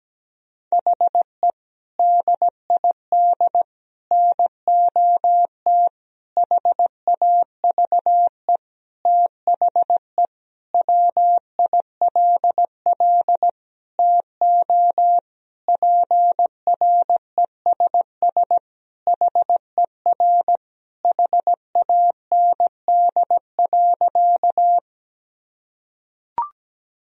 Morse code, heard as a keyed tone, about 17 wpm, 700 Hz. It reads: HE DID NOT HAVE THE WILL TO PRESS HER HAND.
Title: Sentences from Top 1000 Words - Review Part 1 - 17wpm